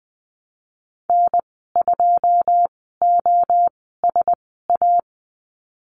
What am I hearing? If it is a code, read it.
Morse code, heard as a keyed tone, about 20 wpm, spaced as for 15 wpm, 700 Hz. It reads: N2OSA
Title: 088 Lesson 15g - Number 5 - Call Signs 20wpm